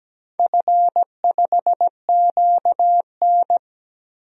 Morse code, heard as a keyed tone, about 17 wpm, 700 Hz. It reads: F5QN
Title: DXCC-100 Call Signs - 4 Characters - 17wpm